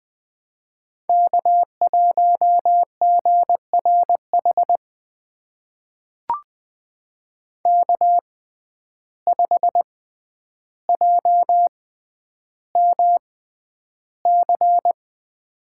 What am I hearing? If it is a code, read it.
Morse code, heard as a keyed tone, about 20 wpm, spaced as for 10 wpm, 700 Hz. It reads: K1GRH K5JMC
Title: USA Call Signs - 6x Character Spacing - 20wpm